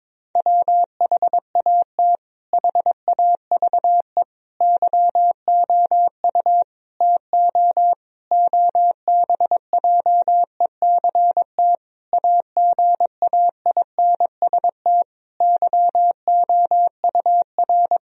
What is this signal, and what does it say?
Morse code, heard as a keyed tone, about 22 wpm, 700 Hz. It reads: WHAT HAVE YOU TO OBJECT AGAINST YOUR
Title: Sentences from Top 1000 Words - Review Part 3 - 22wpm